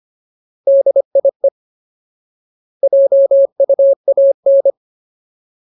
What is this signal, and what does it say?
Morse code, heard as a keyed tone, about 25 wpm, 550 Hz. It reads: DIE JUAN
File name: Ultimate Word List / 25WPM Ultimate